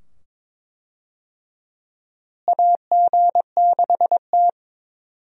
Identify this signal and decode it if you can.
Morse code, heard as a keyed tone, about 22 wpm, 700 Hz. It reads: AG6T
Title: USA Call Signs - 4 Characters - Repeated 2x - 22wpm